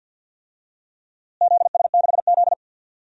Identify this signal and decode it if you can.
Morse code, heard as a keyed tone, about 51 wpm, 700 Hz. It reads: ZS6Z
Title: DXCC-100 Call Signs - 4 Characters - Repeated 2x - 50wpm